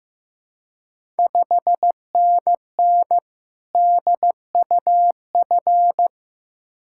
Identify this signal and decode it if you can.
Morse code, heard as a keyed tone, about 15 wpm, 700 Hz. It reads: 5NN DUF
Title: Ontario QSO Party Exchange 15wpm